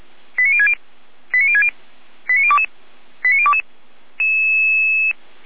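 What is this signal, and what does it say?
Five-tone selective call (ZVEI), heard as quick runs of five tones, tones of about 70 ms. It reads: 89060 89060 89022 89022
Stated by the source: ZVEI five-tone fire brigade call 89123 received on an SDR, logged by Feuerwehr Orlamünde.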